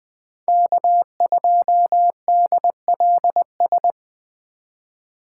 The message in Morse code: K2DLS

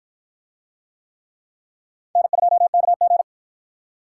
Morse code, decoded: N2KG

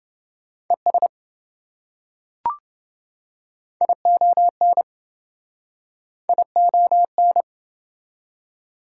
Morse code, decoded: ES ION ION

